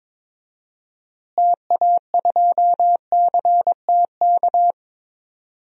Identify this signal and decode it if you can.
Morse code, heard as a keyed tone, about 22 wpm, 700 Hz. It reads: TA2CTK
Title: DXCC-100 Call Signs - 6 Characters - Repeated 3x - 22wpm